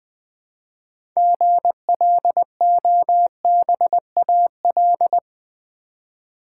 Morse code, heard as a keyed tone, about 20 wpm, 700 Hz. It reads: GLOBAL
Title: Six Letter Words - 20wpm